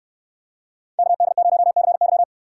Morse code, heard as a keyed tone, about 69 wpm, 700 Hz. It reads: KD0CY